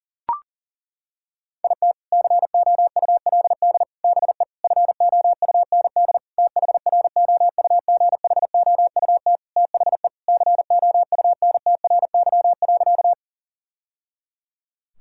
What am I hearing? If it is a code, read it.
Morse code, heard as a keyed tone, about 40 wpm, 700 Hz. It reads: IT COULD BE FOUND THROUGHOUT THE COUNTRY.
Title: Sentences from Top 1000 Words - Review Part 5 - 40wpm